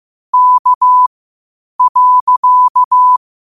Morse code, heard as a keyed tone, about 15 wpm, spaced as for 8 wpm, 1000 Hz. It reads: K.
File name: qotdfiles / 8wpm